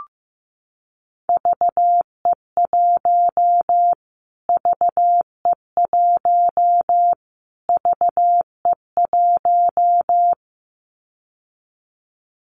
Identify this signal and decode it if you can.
Morse code, heard as a keyed tone, about 15 wpm, 700 Hz. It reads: VE1 VE1 VE1
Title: Callsign Prefixes for Top 25 DXCC - Repeated 3x - 15wpm